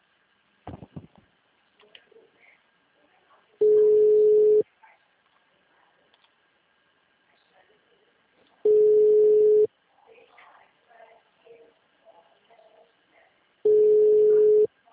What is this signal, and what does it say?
Ringback tone (Europe, 425 Hz).
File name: out- -2021- - - .3943